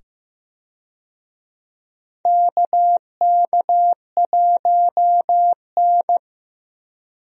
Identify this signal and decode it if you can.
Morse code, heard as a keyed tone, about 15 wpm, 700 Hz. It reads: KK1N
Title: USA Call Signs - 4 Characters - 15wpm